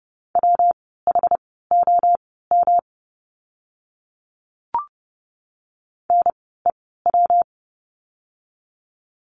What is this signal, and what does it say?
Morse code, heard as a keyed tone, about 30 wpm, spaced as for 17 wpm, 700 Hz. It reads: WHOM NEW
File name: course / 30:10wpm / 115 Lesson 19f - Letter w - Words 30wpm